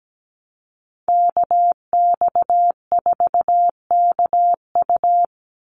Morse code, heard as a keyed tone, about 17 wpm, 700 Hz. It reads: KX4KU
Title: Call Signs Repeated 2x - 17wpm